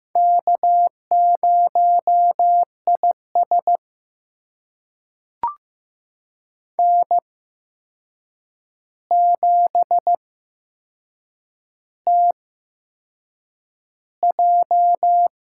Morse code, heard as a keyed tone, about 15 wpm, spaced as for 6 wpm, 700 Hz. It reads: K0IS N7TJ
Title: USA Call Signs - 4 Characters - 8x Character Spacing - 15wpm